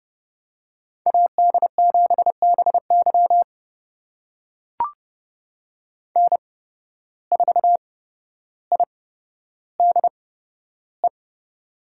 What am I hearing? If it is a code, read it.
Morse code, heard as a keyed tone, about 30 wpm, spaced as for 13 wpm, 700 Hz. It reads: AD7BY N4IDE
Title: USA Call Signs - 5 Characters - 8x Character Spacing - 30wpm